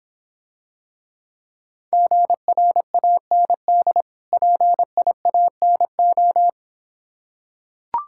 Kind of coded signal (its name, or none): Morse code